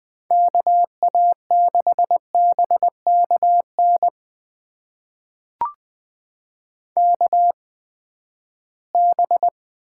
Morse code, heard as a keyed tone, about 20 wpm, spaced as for 14 wpm, 700 Hz. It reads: KA6BKN KB
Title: USA Call Signs - 6 Characters - 8x Character Spacing - 20wpm